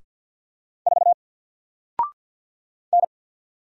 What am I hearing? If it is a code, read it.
Morse code, heard as a keyed tone, about 47 wpm, 700 Hz. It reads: 4 N
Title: Single Letter-Number Alt 50wpm